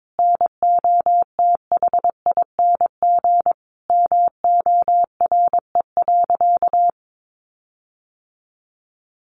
Morse code, heard as a keyed tone, about 22 wpm, 700 Hz. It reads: NOTHING MORE.
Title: Sentences from Top 1400 Words 22wpm